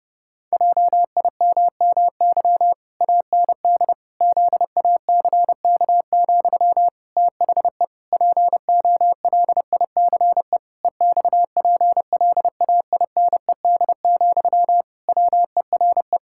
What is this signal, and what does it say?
Morse code, heard as a keyed tone, about 31 wpm, 700 Hz. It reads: JIMMY AND ZACK, THE POLICE EXPLAINED, WERE
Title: Pangram Challenge v2 - 30wpm